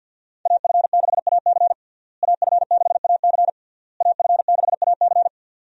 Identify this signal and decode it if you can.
Morse code, heard as a keyed tone, about 52 wpm, 700 Hz. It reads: AF6AC AF6AC AF6AC
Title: USA Call Signs - 5 Characters - Repeated 3x - 50wpm